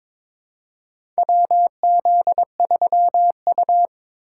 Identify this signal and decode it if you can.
Morse code, heard as a keyed tone, about 22 wpm, 700 Hz. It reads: WZ3U
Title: USA Call Signs - 4 Characters - 2x Character Spacing - 22wpm